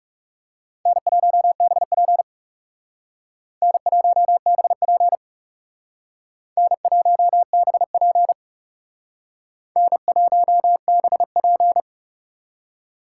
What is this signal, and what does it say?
Morse code, heard as a keyed tone, about 38 wpm, 700 Hz. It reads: N1BP N1BP N1BP N1BP
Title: Speed Racing - USA 1x2 Call Signs 30wpm